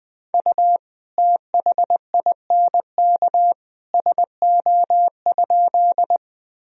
Morse code, heard as a keyed tone, about 21 wpm, 700 Hz. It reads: U THINK SO?
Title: Sentences from Top 200 Words and Abbreviations 20wpm